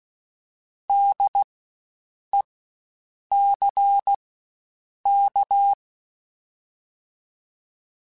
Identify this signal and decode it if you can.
Morse code, heard as a keyed tone, about 16 wpm, spaced as for 7 wpm, 800 Hz. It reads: DECK
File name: Spaced Alphas / Spaced Alphas mp3s / Ds sp @15